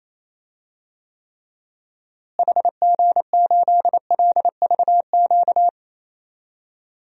Morse code, heard as a keyed tone, about 27 wpm, 700 Hz. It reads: HG8LVQ
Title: DXCC-100 Call Signs - 6 Characters - Repeated 3x - 28wpm